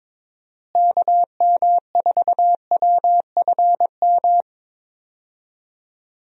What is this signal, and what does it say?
Morse code, heard as a keyed tone, about 22 wpm, 700 Hz. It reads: KM4WFM
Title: USA Call Signs - 6 Characters - Repeated 3x - 22wpm